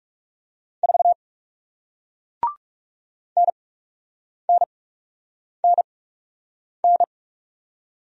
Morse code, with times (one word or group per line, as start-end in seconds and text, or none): 0.8-1.1 4
3.4-7.0 NNNN